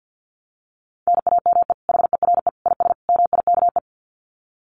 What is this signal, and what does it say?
Morse code, heard as a keyed tone, about 50 wpm, 700 Hz. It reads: NAME HERE IS MIKE
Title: Common QSO Phrases 50wpm